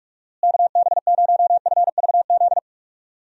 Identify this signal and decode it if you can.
Morse code, heard as a keyed tone, about 44 wpm, 700 Hz. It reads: KB0FVZ
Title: USA Call Signs - 6 Characters - Repeated 3x - 45wpm